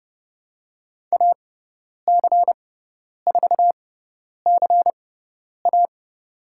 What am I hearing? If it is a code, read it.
Morse code, heard as a keyed tone, about 30 wpm, spaced as for 10 wpm, 700 Hz. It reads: AC4CA
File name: Lesson 16g - Letter c - Call Signs 30wpm